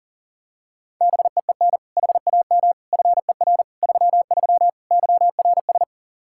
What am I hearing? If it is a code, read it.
Morse code, heard as a keyed tone, about 37 wpm, 700 Hz. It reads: BEEN HAM FER 33 YRS
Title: Practice QSO - Exchange 3 - More Info - 40wpm